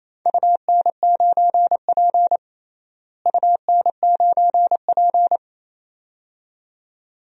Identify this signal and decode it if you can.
Morse code, heard as a keyed tone, about 28 wpm, 700 Hz. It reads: UN9P UN9P
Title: DXCC-100 Call Signs - 4 Characters - Repeated 2x - 28wpm